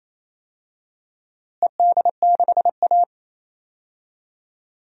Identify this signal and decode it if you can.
Morse code, heard as a keyed tone, about 27 wpm, 700 Hz. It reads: ED6A
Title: DXCC-100 Call Signs - 4 Characters - Repeated 3x - 28wpm